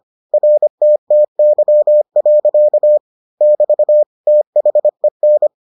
Morse code, heard as a keyed tone, about 25 wpm, 600 Hz. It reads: RTTY. <BT> THEN